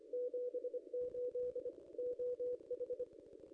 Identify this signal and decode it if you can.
Morse code, heard as a keyed tone, about 26 wpm, 500 Hz. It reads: 78 OH
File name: contests / ss-cw-k5tr-op-n5rz -0700